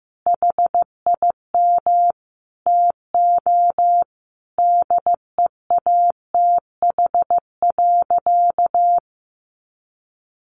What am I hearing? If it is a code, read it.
Morse code, heard as a keyed tone, about 15 wpm, 700 Hz. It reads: HIM TO DEATH.
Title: Sentences from Top 1100 Words 15wpm